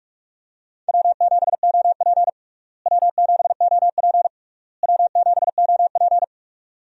Morse code, heard as a keyed tone, about 47 wpm, 700 Hz. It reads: W7OP W7OP W7OP
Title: USA Call Signs - 4 Characters - Repeated 3x - 45wpm